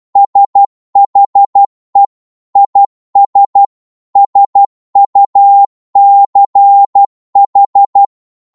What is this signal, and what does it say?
Morse code, heard as a keyed tone, about 12 wpm, 800 Hz. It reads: SHE IS SUCH
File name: Morse / psg-12wpm